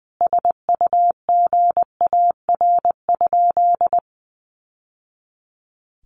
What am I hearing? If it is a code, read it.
Morse code, heard as a keyed tone, about 20 wpm, 700 Hz. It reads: SUGAR?
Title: Sentences from Top 1200 Words 20wpm